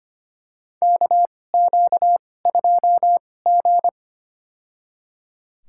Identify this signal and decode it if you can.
Morse code, heard as a keyed tone, about 25 wpm, spaced as for 18 wpm, 700 Hz. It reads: KQ2G
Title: USA Call Signs - 2x Character Spacing - 25wpm